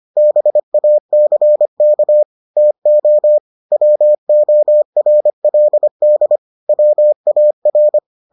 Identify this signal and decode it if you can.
Morse code, heard as a keyed tone, about 25 wpm, 600 Hz. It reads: BACK TO WORLD WAR